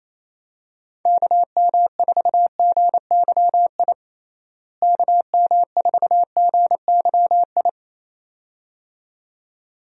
Morse code, heard as a keyed tone, about 28 wpm, 700 Hz. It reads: KM4GYI KM4GYI
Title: USA Call Signs - 6 Characters - Repeated 2x - 28wpm